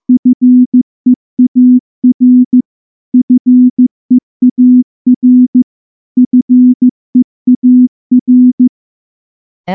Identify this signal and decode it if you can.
Morse code, heard as a keyed tone, about 14 wpm, 250 Hz. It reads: FEAR FEAR FEAR
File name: morse-training-15WPM